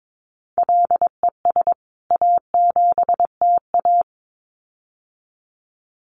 Morse code, heard as a keyed tone, about 22 wpm, 700 Hz. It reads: LES A7TA
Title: CWT Exchange 22wpm